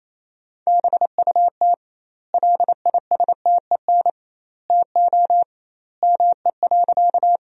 Morse code, heard as a keyed tone, about 28 wpm, 700 Hz. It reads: BUT LISTEN TO ME.
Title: 2x Word-Spacing - Sentences from Top 400 Words - 28wpm